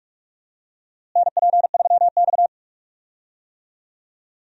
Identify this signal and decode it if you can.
Morse code, heard as a keyed tone, about 47 wpm, 700 Hz. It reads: NP3X
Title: USA Call Signs - 4 Characters - Repeated 2x - 45wpm